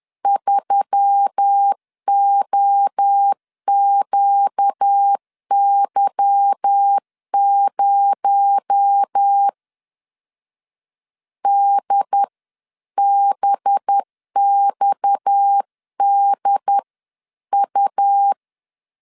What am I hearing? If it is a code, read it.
Morse code, heard as a keyed tone, about 10 wpm, 800 Hz. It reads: 3OQY0 DBXDU